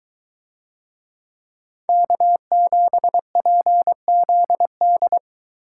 Morse code, heard as a keyed tone, about 22 wpm, 700 Hz. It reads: K7PZD